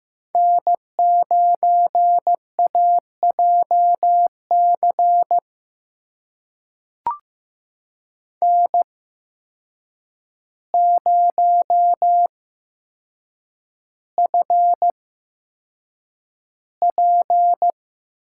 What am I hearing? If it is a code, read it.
Morse code, heard as a keyed tone, about 15 wpm, spaced as for 7 wpm, 700 Hz. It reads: N9AJC N0FP
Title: USA Call Signs - 5 Characters - 8x Character Spacing - 15wpm